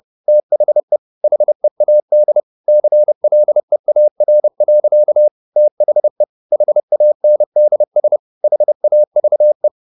Morse code, heard as a keyed tone, about 30 wpm, 600 Hz. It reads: THE HEAD CLEAR. THE HANDS HAVE